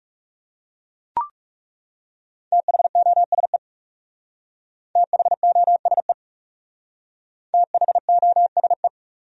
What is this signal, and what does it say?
Morse code, heard as a keyed tone, about 40 wpm, 700 Hz. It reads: THOSE THOSE THOSE